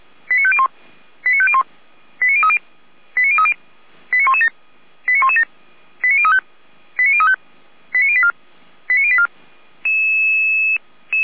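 Five-tone selective call (ZVEI), heard as quick runs of five tones, tones of about 70 ms. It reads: 89501 89501 89030 89030 89117 89117 89035 89035 89084 89084
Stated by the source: ZVEI five-tone selective call received on SDR.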